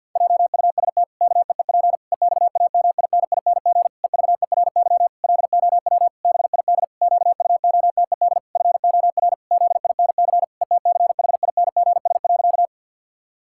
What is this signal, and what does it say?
Morse code, heard as a keyed tone, about 50 wpm, 700 Hz. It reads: JUST KEEP EXAMINING EVERY LOW BID QUOTED FOR ZINC ETCHINGS.
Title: Pangram Challenge v2 - 50wpm